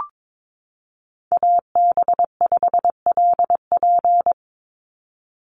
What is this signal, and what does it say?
Morse code, heard as a keyed tone, about 22 wpm, 700 Hz. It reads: AB5LP
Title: USA Call Signs 22wpm